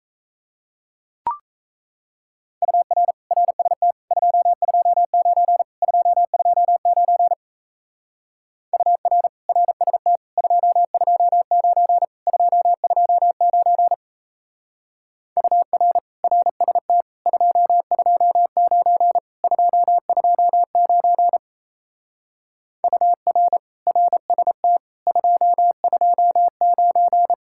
Morse code, heard as a keyed tone, about 33 wpm, 700 Hz. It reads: UR RST 229 229 UR RST 229 229 UR RST 229 229 UR RST 229
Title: Speed-Racing Common QSO Phrases Encore - 28wpm